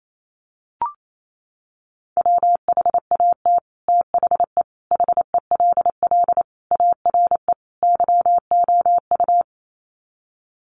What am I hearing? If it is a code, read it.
Morse code, heard as a keyed tone, about 28 wpm, 700 Hz. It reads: WHAT THE HELL ARE YOU
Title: Sets of 5 Words Encore 28wpm